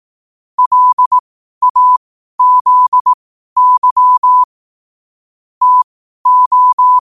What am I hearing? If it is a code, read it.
Morse code, heard as a keyed tone, about 18 wpm, spaced as for 12 wpm, 1000 Hz. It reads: LAZY TO
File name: qotdfiles / 12wpm